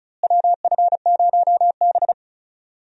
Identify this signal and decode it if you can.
Morse code, heard as a keyed tone, about 34 wpm, 700 Hz. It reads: WF0B
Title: USA Call Signs - 4 Characters - Repeated 3x - 35wpm